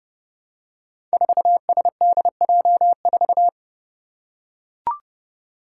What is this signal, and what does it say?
Morse code, heard as a keyed tone, about 30 wpm, 700 Hz. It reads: 4SDJ4
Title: Five Challenging Letter-Number Combinations 30wpm